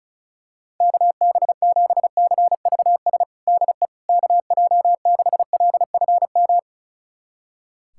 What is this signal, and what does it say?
Morse code, heard as a keyed tone, about 34 wpm, 690 Hz. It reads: KB7CVS DE KJ6LFM
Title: USA Callsign DE Callsign - 6 Characters - 35wpm